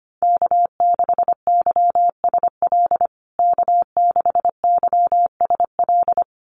Morse code, heard as a keyed tone, about 25 wpm, 700 Hz. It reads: K6YSL K6YSL
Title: USA Call Signs Repeated 2x 25wpm